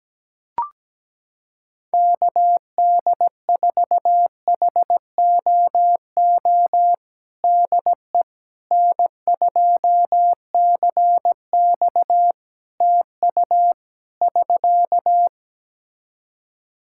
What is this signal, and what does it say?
Morse code, heard as a keyed tone, about 17 wpm, 700 Hz. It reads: KD4HOO DE N2CX TU <SK>
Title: Practice QSO - Exchange 4 - Ending - 17wpm